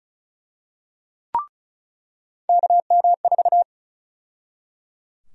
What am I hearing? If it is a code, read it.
Morse code, heard as a keyed tone, about 34 wpm, 700 Hz. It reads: KM4
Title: USA Call Sign Prefixes - 35wpm